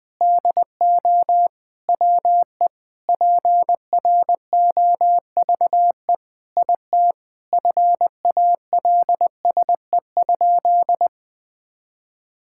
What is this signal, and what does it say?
Morse code, heard as a keyed tone, about 21 wpm, 700 Hz. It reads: DO WE PROVE IT FALSE?
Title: Sentences from Top 1600 Words 20wpm